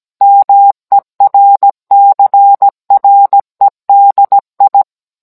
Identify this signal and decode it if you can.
Morse code, heard as a keyed tone, about 17 wpm, 800 Hz. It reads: MERCREDI